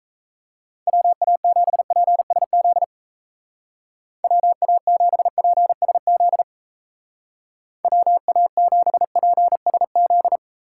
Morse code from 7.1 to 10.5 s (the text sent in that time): WA7PSZ